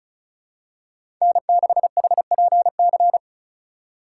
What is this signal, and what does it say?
Morse code, heard as a keyed tone, about 36 wpm, 700 Hz. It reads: N6HPC
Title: USA Call Signs - 5 Characters - 3x Character Spacing - 35wpm